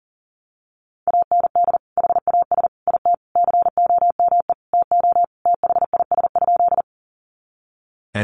Morse code, heard as a keyed tone, about 40 wpm, 700 Hz. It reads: AND HAS IT COME TO THIS?
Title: Sentences from Top 100 Words 40wpm